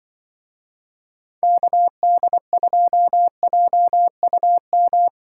Morse code, heard as a keyed tone, about 23 wpm, 700 Hz. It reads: KD2JUM
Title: Speed Racing - USA 2x3 Call Signs 20wpm